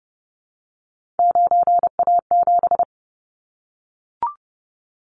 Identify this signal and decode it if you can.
Morse code, heard as a keyed tone, about 30 wpm, 700 Hz. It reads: 9A7
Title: Callsign Prefixes for Top 25 DXCC - 30wpm